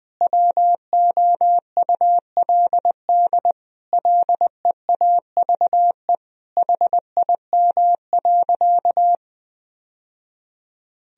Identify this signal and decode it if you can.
Morse code, heard as a keyed tone, about 20 wpm, 700 Hz. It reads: WOULD LEAVE HIM.
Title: Sentences from Top 1000 Words - Review Part 4 - 20wpm